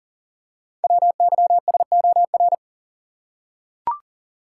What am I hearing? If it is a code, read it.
Morse code, heard as a keyed tone, about 39 wpm, 700 Hz. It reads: WYSOR
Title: Code Groups - 5 Letters - 40wpm